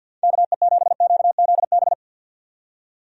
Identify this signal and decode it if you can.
Morse code, heard as a keyed tone, about 48 wpm, 700 Hz. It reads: KE7QZB
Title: USA Call Signs - 6 Characters - 50wpm